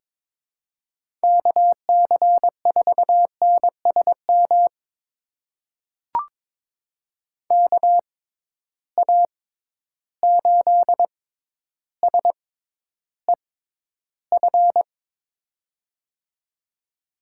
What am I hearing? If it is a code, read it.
Morse code, heard as a keyed tone, about 22 wpm, spaced as for 11 wpm, 700 Hz. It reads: KC4NSM KA8SEF
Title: USA Call Signs - 6 Characters - 6x Character Spacing - 22wpm